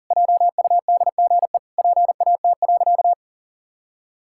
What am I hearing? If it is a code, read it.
Morse code, heard as a keyed tone, about 40 wpm, 700 Hz. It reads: JUDGE PAT.
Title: Pangram Challenge v2 - 40wpm